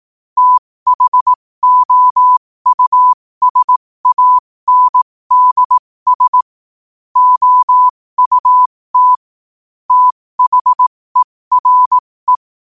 Morse code, heard as a keyed tone, about 18 wpm, spaced as for 15 wpm, 1000 Hz. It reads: THOUSANDS OUT THERE
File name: qotdfiles / 15wpm